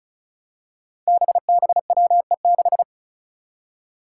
Morse code, heard as a keyed tone, about 36 wpm, 700 Hz. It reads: BBWE6
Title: Five Challenging Letter-Number Combinations 35wpm